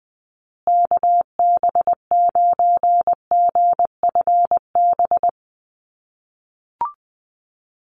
Morse code, heard as a keyed tone, about 20 wpm, 700 Hz. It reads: KB9GFB
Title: USA Call Signs Repeated 2x 20wpm